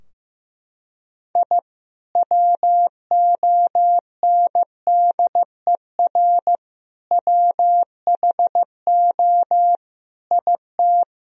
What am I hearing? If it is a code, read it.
Morse code, heard as a keyed tone, about 15 wpm, 700 Hz. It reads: I WONDER WHO IT
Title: Sentences from Top 1000 Words - Review Part 3 - 15wpm